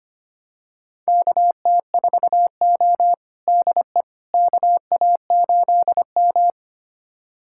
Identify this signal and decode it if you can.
Morse code, heard as a keyed tone, about 25 wpm, 700 Hz. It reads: KT4O DE KA8M
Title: USA Callsign DE Callsign - 4 Characters - 25wpm